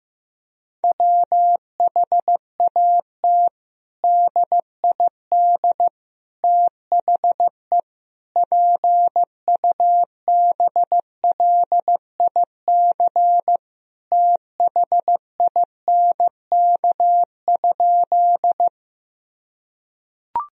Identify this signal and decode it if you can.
Morse code, heard as a keyed tone, about 15 wpm, 700 Hz. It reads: WHAT DID THE PUBLIC THINK?
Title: Sentences from Top 1000 Words - Review Part 3 - 15wpm